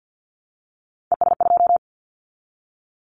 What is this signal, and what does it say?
Morse code, heard as a keyed tone, about 50 wpm, 700 Hz. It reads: ES2
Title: Callsign Prefixes for Top 50 DXCC - 50wpm